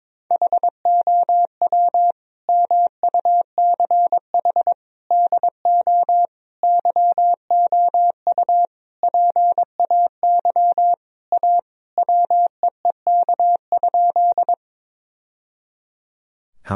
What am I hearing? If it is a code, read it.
Morse code, heard as a keyed tone, about 22 wpm, 700 Hz. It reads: HOW MUCH DO YOU PAY A WEEK?
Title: Sentences from Top 1000 Words - Review Part 2 - 22wpm